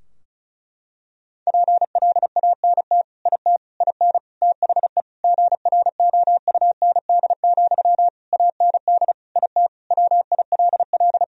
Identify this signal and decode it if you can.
Morse code, heard as a keyed tone, about 35 wpm, 700 Hz. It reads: PLANT IT IN THE GROUND, AND IT WILL
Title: Sentences from Top 1000 Words - Review Part 2 - 35wpm